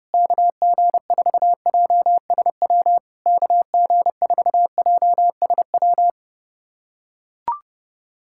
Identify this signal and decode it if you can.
Morse code, heard as a keyed tone, about 30 wpm, 700 Hz. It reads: KG4JSW KG4JSW
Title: SOTA QSO Practice 30wpm